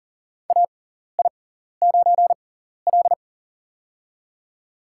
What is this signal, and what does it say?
Morse code, heard as a keyed tone, about 37 wpm, spaced as for 14 wpm, 700 Hz. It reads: AI9L
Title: USA Call Signs - 4 Characters - 6x Character Spacing - 40wpm